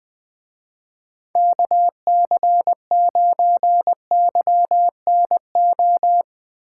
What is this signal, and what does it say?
Morse code, heard as a keyed tone, about 22 wpm, 700 Hz. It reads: KC9YNO
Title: Speed Racing - USA 2x3 Call Signs 20wpm